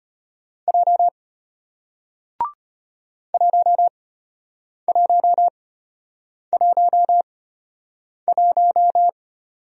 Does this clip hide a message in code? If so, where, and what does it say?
Yes: Morse code from 0.7 to 9.1 s, heard as a keyed tone, about 33 wpm, spaced as for 8 wpm, 700 Hz. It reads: J 1111